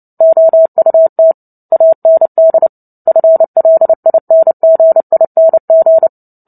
Morse code, heard as a keyed tone, about 30 wpm, 650 Hz. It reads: OUT AND FLINGING
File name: Book Two / 03-Chapter 3 - 29 WPM